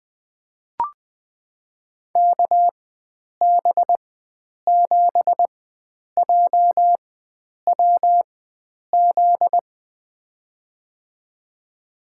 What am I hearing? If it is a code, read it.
Morse code, heard as a keyed tone, about 20 wpm, spaced as for 9 wpm, 700 Hz. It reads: KB7JWZ
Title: USA Call Signs - 6 Characters - 4x Character Spacing - 20wpm